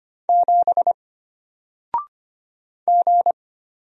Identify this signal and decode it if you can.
Morse code, heard as a keyed tone, about 25 wpm, 700 Hz. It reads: 7 G
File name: Single-Letter-Number-25wpm